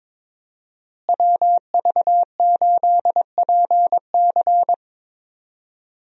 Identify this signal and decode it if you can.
Morse code, heard as a keyed tone, about 22 wpm, 700 Hz. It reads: WV8PC
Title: USA Call Signs - 5 Characters - Repeated 3x - 22wpm